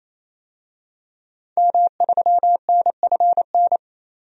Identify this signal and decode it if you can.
Morse code, heard as a keyed tone, about 28 wpm, 700 Hz. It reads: M3NFN
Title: DXCC-100 Call Signs - 5 Characters - 28wpm